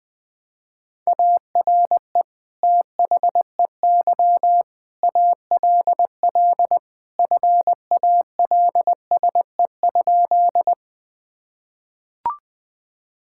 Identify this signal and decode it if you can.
Morse code, heard as a keyed tone, about 20 wpm, 700 Hz. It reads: ARE THEY ALL FALSE?
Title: Sentences from Top 1600 Words 20wpm